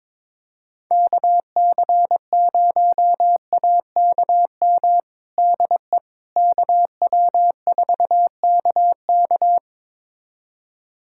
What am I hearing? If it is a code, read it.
Morse code, heard as a keyed tone, about 22 wpm, 700 Hz. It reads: KC0AKM DE KW4KK
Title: USA Callsign DE Callsign 22wpm